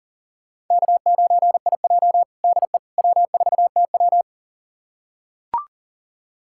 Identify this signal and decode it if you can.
Morse code, heard as a keyed tone, about 40 wpm, 700 Hz. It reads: K9IJ DE W4TW
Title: USA Callsign DE Callsign - 4 Characters - 40wpm